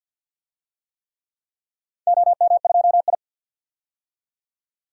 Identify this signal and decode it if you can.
Morse code, heard as a keyed tone, about 45 wpm, 700 Hz. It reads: OM2I
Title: DXCC-100 Call Signs - 4 Characters - 50wpm